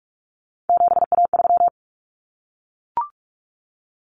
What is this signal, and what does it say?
Morse code, heard as a keyed tone, about 44 wpm, 700 Hz. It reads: 7A3
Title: Common Cut Numbers - 3-Digit 45wpm